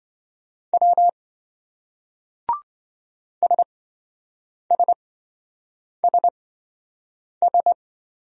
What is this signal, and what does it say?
Morse code, heard as a keyed tone, about 26 wpm, spaced as for 7 wpm, 700 Hz. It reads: W SSSS